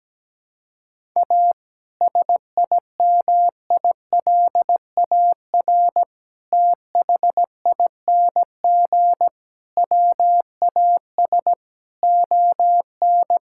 Morse code, heard as a keyed tone, about 17 wpm, 700 Hz. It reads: A SIMILAR THING WAS ON
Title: Sentences from Top 1000 Words - Review Part 5 - 17wpm